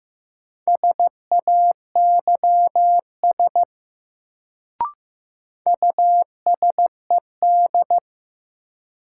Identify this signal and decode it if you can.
Morse code, heard as a keyed tone, about 15 wpm, 700 Hz. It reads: SAYS USED
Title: Sets of 2 Words - New Word Drill - 15wpm